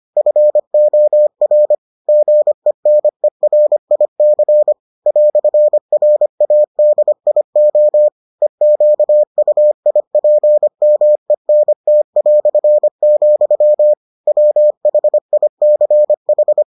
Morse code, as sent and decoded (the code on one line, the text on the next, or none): ..-. --- .-. / --. -. .-. .. -.-. / .-..-. .-. .- -.. .. --- / . --.- ..- .. .--. -- . -. - .-..-. --..-- / .-- .... .. -.-. ....
FOR GENERIC "RADIO EQUIPMENT", WHICH